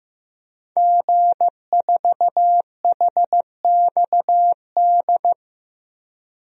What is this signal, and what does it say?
Morse code, heard as a keyed tone, about 15 wpm, 700 Hz. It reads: G4HXD
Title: DXCC-100 Call Signs - 5 Characters - Repeated 2x - 15wpm